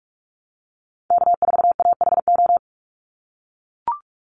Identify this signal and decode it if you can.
Morse code, heard as a keyed tone, about 44 wpm, 700 Hz. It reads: K4AHO
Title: Call Signs - 45wpm